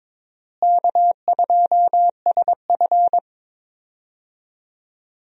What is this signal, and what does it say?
Morse code, heard as a keyed tone, about 22 wpm, 700 Hz. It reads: K2SF
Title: USA Call Signs - 4 Characters - Repeated 3x - 22wpm